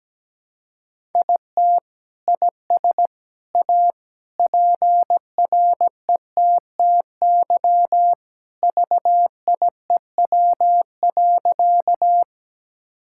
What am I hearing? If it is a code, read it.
Morse code, heard as a keyed tone, about 17 wpm, 700 Hz. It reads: IT IS A PRETTY VIEW.
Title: Sentences from Top 1000 Words - Review Part 6 - 17wpm